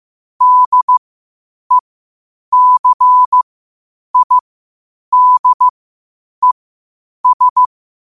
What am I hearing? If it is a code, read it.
Morse code, heard as a keyed tone, about 15 wpm, spaced as for 8 wpm, 1000 Hz. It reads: DECIDES
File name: qotdfiles / 8wpm